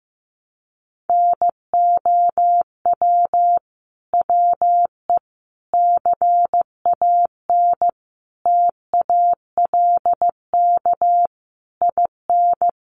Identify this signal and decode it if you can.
Morse code, heard as a keyed tone, about 15 wpm, 700 Hz. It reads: NOW WE CAN TALK IN
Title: Sentences from Top 700 Words 15wpm